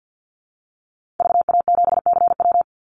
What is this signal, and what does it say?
Morse code, heard as a keyed tone, about 50 wpm, 700 Hz. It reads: VA7CW
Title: Call Signs - 50wpm